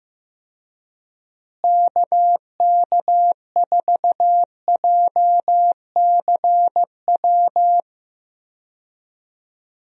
Morse code, heard as a keyed tone, about 16 wpm, 700 Hz. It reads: KK4JCW
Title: USA Call Signs - 6 Characters - Repeated 2x - 15wpm